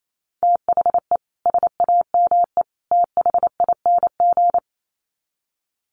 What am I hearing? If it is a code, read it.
Morse code, heard as a keyed tone, about 28 wpm, 700 Hz. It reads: THE SAME THING
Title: Sets of 3 Words 28wpm